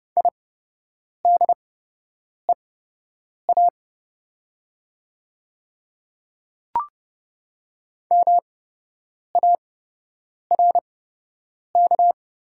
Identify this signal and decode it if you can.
Morse code, heard as a keyed tone, about 30 wpm, spaced as for 8 wpm, 700 Hz. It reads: IDEA MARK
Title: 8x Character Spacing - Top 300 Words - 30wpm